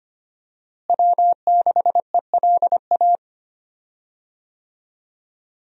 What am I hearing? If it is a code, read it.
Morse code, heard as a keyed tone, about 25 wpm, 700 Hz. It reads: W6ELA
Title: USA Call Signs - 5 Characters - Repeated 3x - 25wpm